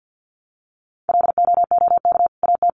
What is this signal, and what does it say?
Morse code, heard as a keyed tone, about 50 wpm, 700 Hz. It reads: LOOK AT
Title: Sentences from Top 1500 Words 50wpm